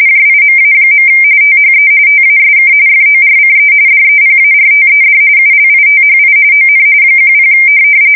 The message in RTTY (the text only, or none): JIMA. THE TASK FORCE DEPARTED ULITHI ON 10